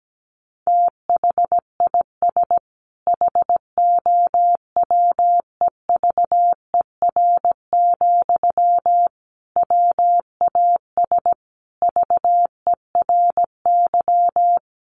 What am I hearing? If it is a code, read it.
Morse code, heard as a keyed tone, about 17 wpm, 700 Hz. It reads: THIS HOWEVER, WAS VERY
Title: Sentences from Top 1400 Words 17wpm